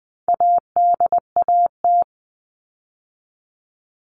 Morse code, heard as a keyed tone, about 20 wpm, 700 Hz. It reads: ADAT
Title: All Cut Numbers - 4-Digit 20wpm